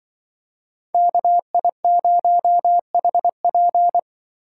Morse code, heard as a keyed tone, about 24 wpm, 700 Hz. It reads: KI0HP
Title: Speed Racing - USA 2x2 Call Signs 20wpm